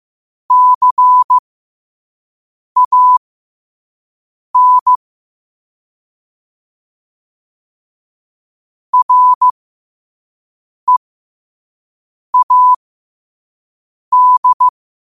Morse code, heard as a keyed tone, about 15 wpm, spaced as for 5 wpm, 1000 Hz. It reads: CAN READ